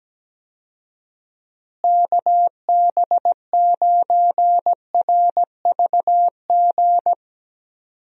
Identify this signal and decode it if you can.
Morse code, heard as a keyed tone, about 17 wpm, 700 Hz. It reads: KB9RVG
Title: USA Call Signs - 6 Characters - Repeated 2x - 17wpm